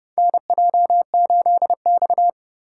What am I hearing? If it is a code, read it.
Morse code, heard as a keyed tone, about 32 wpm, 700 Hz. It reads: NJ8X
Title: USA Call Signs - 4 Characters - 3x Character Spacing - 30wpm